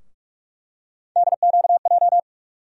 Morse code, heard as a keyed tone, about 42 wpm, 700 Hz. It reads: DQJ
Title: USA Call Sign Postfixes - 3 Letters 45wpm